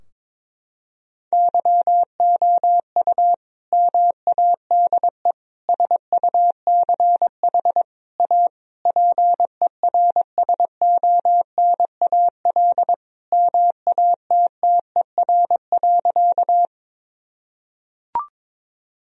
Morse code, heard as a keyed tone, about 22 wpm, 700 Hz. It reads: YOU MADE SUCH A PERSONAL MATTER.
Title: Sentences from Top 1000 Words - Review Part 4 - 22wpm